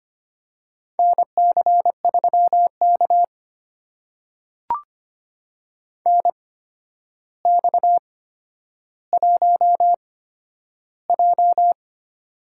Morse code, heard as a keyed tone, about 25 wpm, spaced as for 11 wpm, 700 Hz. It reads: NC3K NX1J
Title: USA Call Signs - 4 Characters - 8x Character Spacing - 25wpm